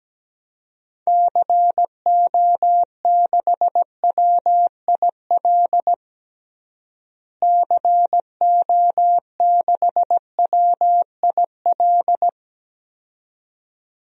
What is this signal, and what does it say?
Morse code, heard as a keyed tone, about 17 wpm, 700 Hz. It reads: CO6WIL CO6WIL
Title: DXCC-100 Call Signs - 6 Characters - Repeated 2x - 17wpm